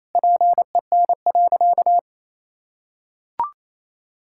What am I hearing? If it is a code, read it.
Morse code, heard as a keyed tone, about 28 wpm, 700 Hz. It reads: PEN.
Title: Pangram Challenge v2 - 28wpm